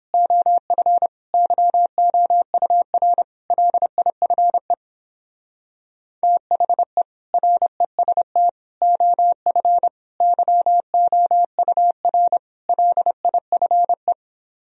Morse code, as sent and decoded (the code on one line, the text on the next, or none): --- ..-. / -.-- --- ..- .-. / .-.. .. ..-. / - .... / .-. ... - / --- ..-. / -.-- --- ..- .-. / .-.. .. ..-. .
OF YOUR LIFE THE REST OF YOUR LIFE